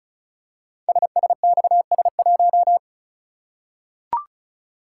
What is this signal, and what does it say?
Morse code, heard as a keyed tone, about 36 wpm, 700 Hz. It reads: SSXS1
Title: Code Groups - 5 Letter:Numbers - 35wpm